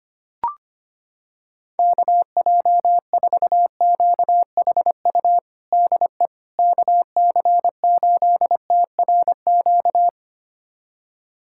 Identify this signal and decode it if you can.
Morse code, heard as a keyed tone, about 25 wpm, 700 Hz. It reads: KJ4QHU DE KC8TRQ
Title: USA Callsign DE Callsign 25wpm